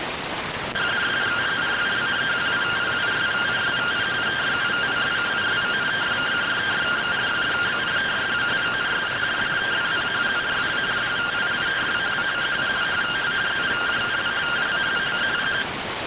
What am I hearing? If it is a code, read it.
Two-tone teleprinter signal, about 170 Hz shift.